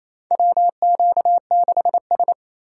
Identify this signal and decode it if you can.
Morse code, heard as a keyed tone, about 28 wpm, 700 Hz. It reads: WQ6S
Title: USA Call Signs - 4x Character Spacing - 28wpm